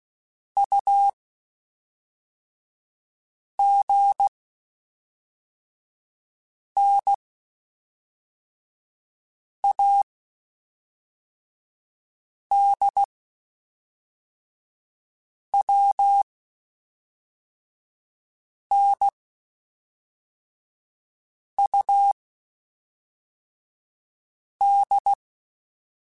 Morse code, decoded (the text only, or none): U G N A D W N U D